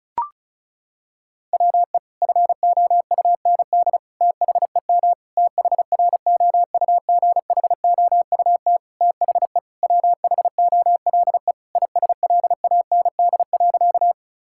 Morse code, heard as a keyed tone, about 34 wpm, 700 Hz. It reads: WE FOUND THEM THROUGHOUT THE WHOLE ISLAND.
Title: Sentences from Top 1000 Words - Review Part 5 - 35wpm